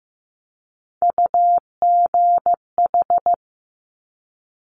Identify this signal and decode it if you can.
Morse code, heard as a keyed tone, about 15 wpm, 700 Hz. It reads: UGH
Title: Top Three Letter Combinations in Words 15wpm